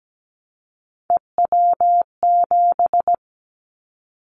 Morse code, heard as a keyed tone, about 17 wpm, 700 Hz. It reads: EW7
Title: Callsign Prefixes for Top 100 DXCC - 17wpm